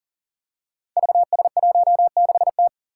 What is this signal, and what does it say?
Morse code, heard as a keyed tone, about 40 wpm, 700 Hz. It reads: VS16T
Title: Five Challenging Letter-Number Combinations 40wpm